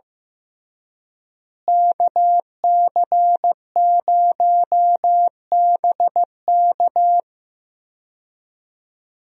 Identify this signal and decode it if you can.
Morse code, heard as a keyed tone, about 15 wpm, 700 Hz. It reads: KC0BK
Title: USA Call Signs - 5 Characters - Repeated 3x - 15wpm